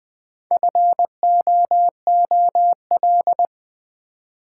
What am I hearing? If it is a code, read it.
Morse code, heard as a keyed tone, about 20 wpm, 700 Hz. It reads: FOOL
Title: Build A Word - 1 Letter At A Time - 4-Letter Words - 20wpm